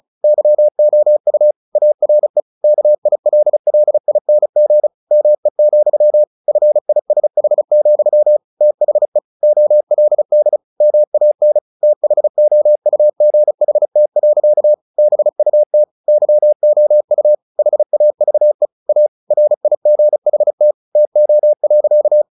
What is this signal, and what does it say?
Morse code, heard as a keyed tone, about 34 wpm, 600 Hz. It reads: YOU ARE KILLING ME, FISH, THE OLD MAN THOUGHT. BUT YOU HAVE A RIGHT TO.
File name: the old man and the sea 2 35wpm